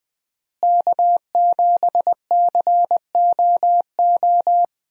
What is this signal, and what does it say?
Morse code, heard as a keyed tone, about 21 wpm, 700 Hz. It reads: K7COO
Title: USA Call Signs - 5 Characters - Repeated 3x - 20wpm